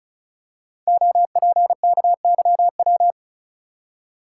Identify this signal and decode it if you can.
Morse code, heard as a keyed tone, about 36 wpm, 690 Hz. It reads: OPKYW
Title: Code Groups - 5 Letter:Numbers - 35wpm